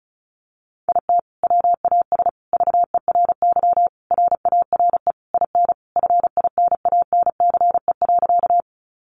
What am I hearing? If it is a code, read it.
Morse code, heard as a keyed tone, about 34 wpm, 700 Hz. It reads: IT WAS VERY RARE IN FINANCE.
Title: Sentences from Top 1500 Words 35wpm